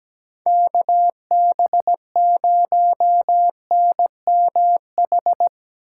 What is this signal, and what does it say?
Morse code, heard as a keyed tone, about 20 wpm, 700 Hz. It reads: KB0NMH
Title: Speed Racing - USA 2x3 Call Signs 17wpm